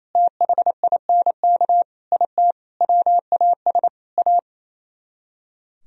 Morse code, heard as a keyed tone, about 34 wpm, 700 Hz. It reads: THINK IT WAS A
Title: Speed-Racing - Sets of 5 Words 28wpm